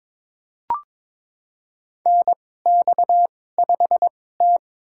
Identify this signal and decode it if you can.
Morse code, heard as a keyed tone, about 22 wpm, spaced as for 16 wpm, 700 Hz. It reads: NX5T
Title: USA Call Signs - 4 Characters - 2x Character Spacing - 22wpm